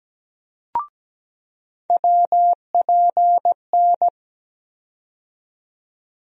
Morse code, heard as a keyed tone, about 17 wpm, 700 Hz. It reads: WPN
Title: USA Call Sign Postfixes - 3 Letters 17wpm